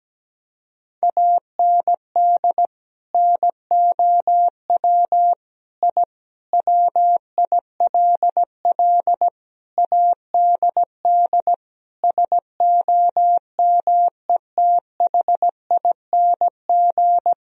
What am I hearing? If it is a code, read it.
Morse code, heard as a keyed tone, about 17 wpm, 700 Hz. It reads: AND NOW I WILL ADD SOMETHING
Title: Sentences from Top 1000 Words - Review Part 1 - 17wpm